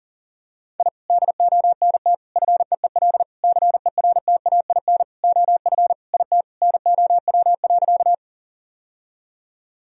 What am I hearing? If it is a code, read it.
Morse code, heard as a keyed tone, about 40 wpm, 700 Hz. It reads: I DONT FEEL CERTAIN OF IT NOW.
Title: Sentences from Top 1000 Words - Review Part 2 - 40wpm